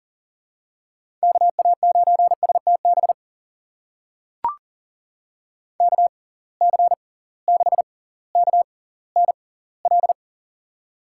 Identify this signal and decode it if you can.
Morse code, heard as a keyed tone, about 40 wpm, spaced as for 20 wpm, 700 Hz. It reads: KA9STB KC6KNL
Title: USA Call Signs - 6 Characters - 6x Character Spacing - 40wpm